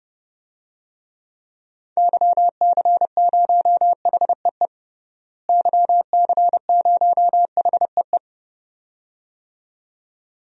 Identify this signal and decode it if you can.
Morse code, heard as a keyed tone, about 31 wpm, 700 Hz. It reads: YC0HEE YC0HEE
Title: DXCC-100 Call Signs - 6 Characters - Repeated 2x - 30wpm